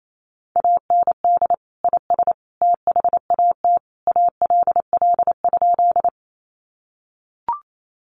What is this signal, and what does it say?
Morse code, heard as a keyed tone, about 28 wpm, 700 Hz. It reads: AND IS THAT ALL?
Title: Sentences from Top 100 Words 28wpm